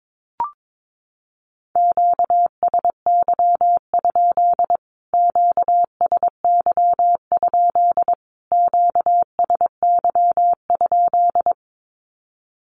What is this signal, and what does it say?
Morse code, heard as a keyed tone, about 22 wpm, 700 Hz. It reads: QSY? QSY? QSY?
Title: Q-Codes Repeated 3x 22wpm